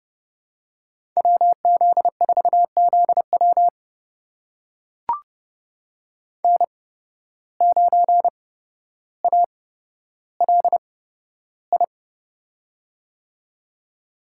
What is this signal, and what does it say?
Morse code, heard as a keyed tone, about 31 wpm, spaced as for 13 wpm, 700 Hz. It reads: WZ4ZW N9ALI
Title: USA Call Signs - 5 Characters - 8x Character Spacing - 30wpm